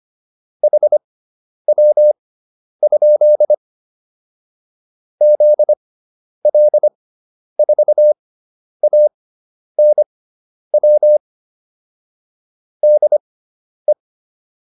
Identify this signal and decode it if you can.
Morse code, heard as a keyed tone, about 25 wpm, spaced as for 10 wpm, 600 Hz. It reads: HW? ZL4ANW DE
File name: QSO404 10